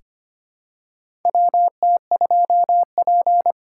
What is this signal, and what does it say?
Morse code, heard as a keyed tone, about 25 wpm, 700 Hz. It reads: WT2P
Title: USA Call Signs - 4 Characters - 3x Character Spacing - 25wpm